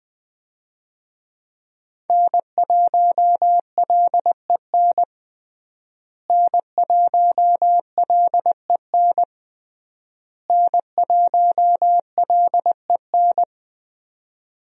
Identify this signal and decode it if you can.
Morse code, heard as a keyed tone, about 21 wpm, 700 Hz. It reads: N1LEN N1LEN N1LEN
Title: USA Call Signs - 5 Characters - Repeated 3x - 20wpm